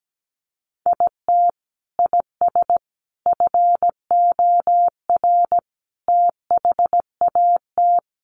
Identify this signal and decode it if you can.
Morse code, heard as a keyed tone, about 17 wpm, 700 Hz. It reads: IT IS FOR THAT
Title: Sentences from Top 700 Words 17wpm